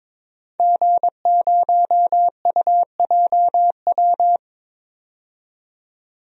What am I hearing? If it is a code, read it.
Morse code, heard as a keyed tone, about 22 wpm, 700 Hz. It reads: G0UJW
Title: DXCC-100 Call Signs - 5 Characters - 22wpm